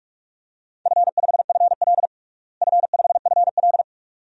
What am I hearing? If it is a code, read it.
Morse code, heard as a keyed tone, about 46 wpm, 700 Hz. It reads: F5FL F5FL
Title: DXCC-100 Call Signs - 4 Characters - Repeated 2x - 45wpm